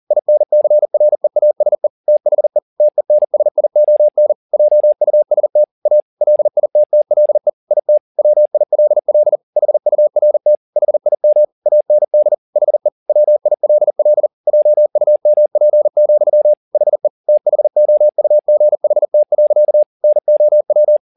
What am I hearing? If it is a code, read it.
Morse code, heard as a keyed tone, about 40 wpm, 600 Hz. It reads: INCREASE THE TENSION JUST A LITTLE IT WILL HURT HIM AND HE WILL JUMP, HE THOUGHT. NOW